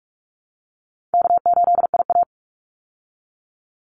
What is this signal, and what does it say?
Morse code, heard as a keyed tone, about 44 wpm, 700 Hz. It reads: K8IA
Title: Easy Call Signs Repeated 3x - 45wpm